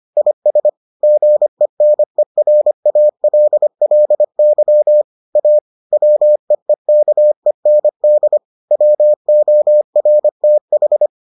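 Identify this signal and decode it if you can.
Morse code, heard as a keyed tone, about 25 wpm, 600 Hz. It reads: IS GENERALLY A WEEKEND WORTH
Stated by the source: Morse edition of the Foundations of Amateur Radio episodes.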